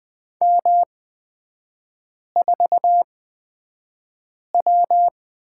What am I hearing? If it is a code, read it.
Morse code, heard as a keyed tone, about 20 wpm, spaced as for 5 wpm, 700 Hz. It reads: M4W
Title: 236 Lesson 38e - Letter q - Three Characters 20wpm